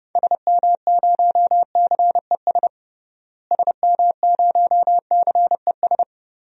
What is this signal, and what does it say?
Morse code, heard as a keyed tone, about 28 wpm, 700 Hz. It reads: SM0CES SM0CES
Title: DXCC-100 Call Signs - 6 Characters - Repeated 2x - 30wpm